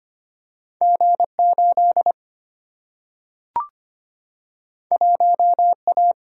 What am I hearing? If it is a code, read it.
Morse code, heard as a keyed tone, about 25 wpm, 700 Hz. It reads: G 8 1 A
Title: Single-Letter Single-Number 25wpm